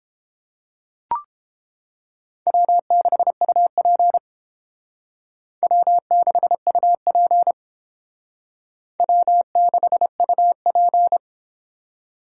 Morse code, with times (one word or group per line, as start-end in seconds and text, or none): 2.5-4.2 W6UP
5.6-7.5 W6UP
9.0-11.2 W6UP